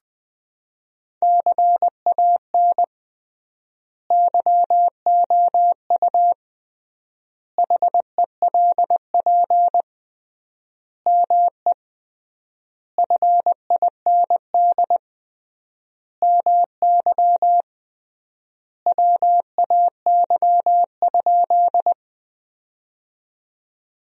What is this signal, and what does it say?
Morse code, heard as a keyed tone, about 21 wpm, 700 Hz. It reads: CAN YOU HELP ME FIND MY WAY?
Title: 3x Word-Spacing - Sentences from Top 200 Words - 20wpm